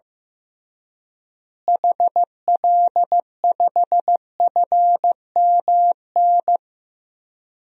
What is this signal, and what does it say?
Morse code, heard as a keyed tone, about 15 wpm, 700 Hz. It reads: HL5FMN
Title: DXCC-100 Call Signs - 6 Characters - Repeated 2x - 15wpm